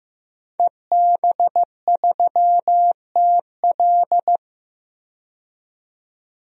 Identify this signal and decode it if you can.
Morse code, heard as a keyed tone, about 15 wpm, 700 Hz. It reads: EB3TL
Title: DXCC-100 Call Signs - 5 Characters - 15wpm